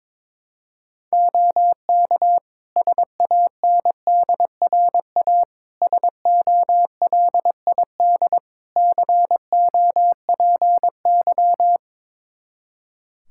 Morse code, heard as a keyed tone, about 22 wpm, 700 Hz. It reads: OK SANDRA SOLID COPY
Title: Practice QSO - Exchange 3 - More Info - 22wpm